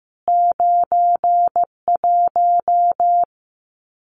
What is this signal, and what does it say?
Morse code, heard as a keyed tone, about 15 wpm, 700 Hz. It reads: 91
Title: CWT Exchange 15wpm